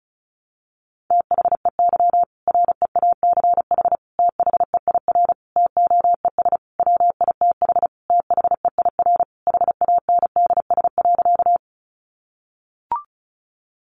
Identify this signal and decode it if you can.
Morse code, heard as a keyed tone, about 34 wpm, 700 Hz. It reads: THEY REACH THEIR TOES WITH THEIR HANDS.
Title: Easy Phrases 35wpm